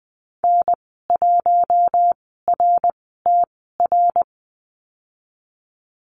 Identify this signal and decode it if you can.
Morse code, heard as a keyed tone, about 20 wpm, spaced as for 14 wpm, 700 Hz. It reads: N1RTR
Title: 062 Lesson 11g - Letter h - Call Signs 20wpm